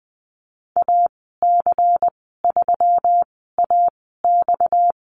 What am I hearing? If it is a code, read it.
Morse code, heard as a keyed tone, about 20 wpm, spaced as for 15 wpm, 700 Hz. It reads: AC3AX
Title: 231 Lesson 37g - Letter x - Call Signs 20wpm